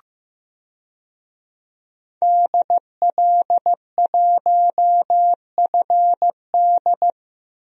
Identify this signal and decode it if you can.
Morse code, heard as a keyed tone, about 15 wpm, 700 Hz. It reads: DL1FD
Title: DXCC-100 Call Signs - 5 Characters - 15wpm